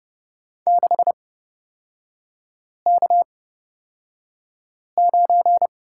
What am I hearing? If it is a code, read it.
Morse code, heard as a keyed tone, about 31 wpm, spaced as for 5 wpm, 700 Hz. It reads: 6K9
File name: Lesson 37e - Letter x - Three Characters 30wpm